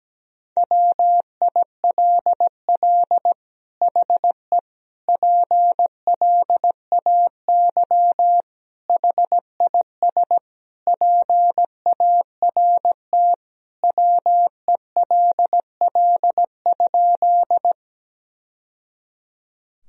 Morse code, heard as a keyed tone, about 17 wpm, 700 Hz. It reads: WILL HE PLAY HIS PART WELL?
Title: Sentences from Top 1000 Words - Review Part 1 - 17wpm